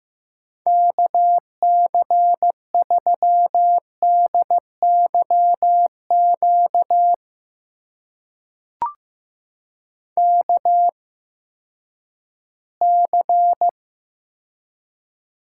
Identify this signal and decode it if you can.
Morse code, heard as a keyed tone, about 15 wpm, spaced as for 10 wpm, 700 Hz. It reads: KC3DYQ KC
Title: USA Call Signs - 6 Characters - 8x Character Spacing - 15wpm